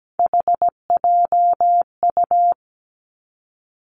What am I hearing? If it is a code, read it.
Morse code, heard as a keyed tone, about 17 wpm, 700 Hz. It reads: HJU
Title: Three Challenging Letter-Number Combinations 17wpm